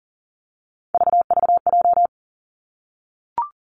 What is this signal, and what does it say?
Morse code, heard as a keyed tone, about 41 wpm, 700 Hz. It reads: VVJ
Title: Three Challenging Letter-Number Combinations 40wpm